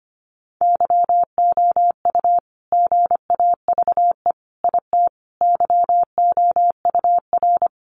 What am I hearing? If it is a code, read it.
Morse code, heard as a keyed tone, about 25 wpm, 700 Hz. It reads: YOU GAVE IT YOUR
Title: Sentences from Top 1500 Words 25wpm